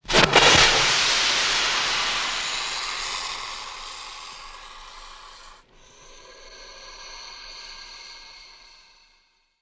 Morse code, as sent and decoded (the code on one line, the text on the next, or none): none